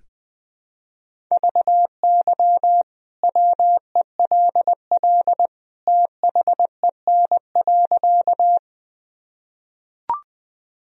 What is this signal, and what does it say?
Morse code, heard as a keyed tone, about 20 wpm, 700 Hz. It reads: VY WELL THEN.
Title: Sentences from Top 200 Words and Abbreviations 20wpm